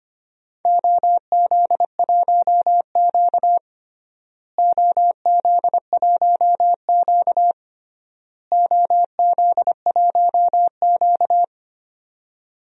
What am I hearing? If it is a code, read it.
Morse code, heard as a keyed tone, about 25 wpm, 700 Hz. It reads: OZ1Q OZ1Q OZ1Q
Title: DXCC-100 Call Signs - 4 Characters - Repeated 3x - 25wpm